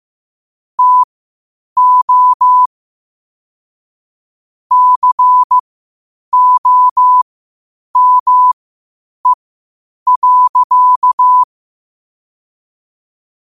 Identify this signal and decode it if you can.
Morse code, heard as a keyed tone, about 15 wpm, spaced as for 8 wpm, 1000 Hz. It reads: TO COME.
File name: qotdfiles / 8wpm